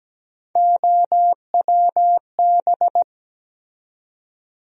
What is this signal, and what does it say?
Morse code, heard as a keyed tone, about 17 wpm, 700 Hz. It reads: OWB
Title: USA Call Sign Postfixes - 3 Letters 17wpm